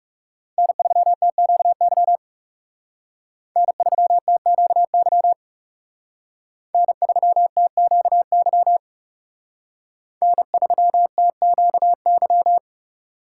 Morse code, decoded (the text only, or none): N3TQY N3TQY N3TQY N3TQY